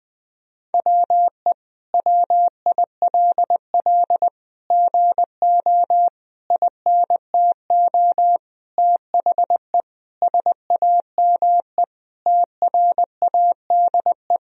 Morse code, heard as a keyed tone, about 20 wpm, 700 Hz. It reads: WE WILL GO INTO THE SAME TRADE
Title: Sentences from Top 1000 Words - Review Part 4 - 20wpm